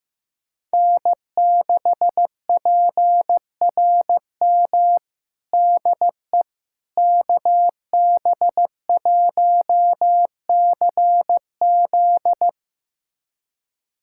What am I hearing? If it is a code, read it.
Morse code, heard as a keyed tone, about 15 wpm, 700 Hz. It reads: N6PRM DE KB1CZ
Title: USA Callsign DE Callsign - 5 Characters - 15wpm